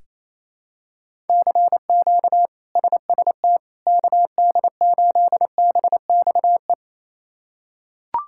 Morse code, heard as a keyed tone, about 28 wpm, 700 Hz. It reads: CQ SST KD8BXE
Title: K1USN SST QSO Practice - 28wpm